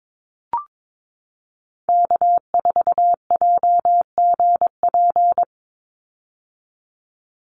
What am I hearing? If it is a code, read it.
Morse code, heard as a keyed tone, about 22 wpm, 700 Hz. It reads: K4JGP
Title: USA Call Signs 22wpm